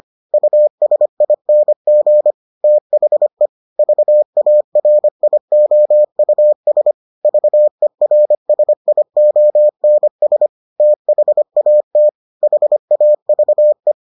Morse code, heard as a keyed tone, about 25 wpm, 600 Hz. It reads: USING THE VARIOUS VERSIONS THAT HAVE